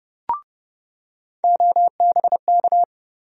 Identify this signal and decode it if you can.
Morse code, heard as a keyed tone, about 32 wpm, 700 Hz. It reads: OBK
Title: USA Call Sign Postfixes - 3 Letters 30wpm